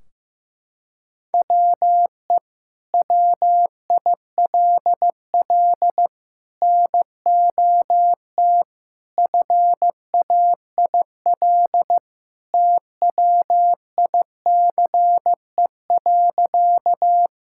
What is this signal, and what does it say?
Morse code, heard as a keyed tone, about 15 wpm, 700 Hz. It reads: WE WILL NOT FAIL TWICE.
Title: Sentences from Top 1600 Words 15wpm